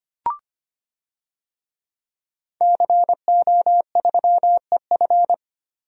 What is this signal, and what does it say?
Morse code, heard as a keyed tone, about 25 wpm, 700 Hz. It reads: CO3EF
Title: DXCC-100 Call Signs - 5 Characters - 25wpm